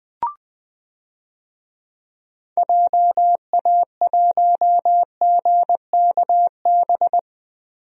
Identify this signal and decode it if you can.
Morse code, heard as a keyed tone, about 20 wpm, 700 Hz. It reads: JA1GKB
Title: DXCC-100 Call Signs - 6 Characters - Repeated 2x - 20wpm